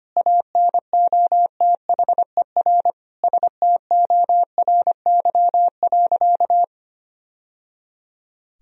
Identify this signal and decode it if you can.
Morse code, heard as a keyed tone, about 25 wpm, 700 Hz. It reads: ANOTHER STORY.
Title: Sentences from Top 1000 Words - Review Part 1 - 25wpm